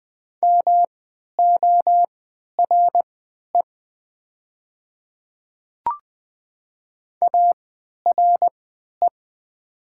Morse code, heard as a keyed tone, about 20 wpm, spaced as for 11 wpm, 700 Hz. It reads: MORE ARE
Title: 3x Character Spacing - Top 100 Words - 20wpm